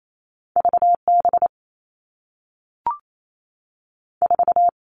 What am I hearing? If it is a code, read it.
Morse code, heard as a keyed tone, about 28 wpm, 700 Hz. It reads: V B 4
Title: Two Challenging Letter-Number Combinations 28wpm